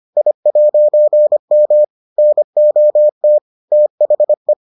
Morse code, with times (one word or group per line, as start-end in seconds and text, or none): 0.2-1.8 I'M
2.2-3.4 NOT
3.7-4.5 THE